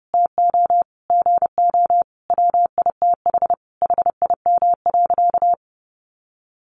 Morse code, heard as a keyed tone, about 30 wpm, 700 Hz. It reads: TO GO WITH HIM.
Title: Sentences from Top 1300 Words 30wpm